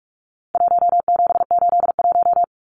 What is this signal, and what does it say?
Morse code, heard as a keyed tone, about 44 wpm, 700 Hz. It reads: 1781
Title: Numbers - Contest Style - 4-Digit 45wpm